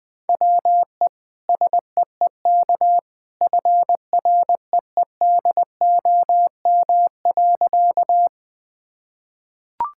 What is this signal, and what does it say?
Morse code, heard as a keyed tone, about 20 wpm, 700 Hz. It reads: WE SEEK FREEDOM.
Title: Sentences from Top 1600 Words 20wpm